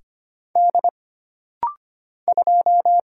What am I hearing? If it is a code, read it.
Morse code, heard as a keyed tone, about 25 wpm, 700 Hz. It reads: D 2
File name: Single Character (Letter:Number:Punctuation:Prosign) - Rapid-Fire - 25wpm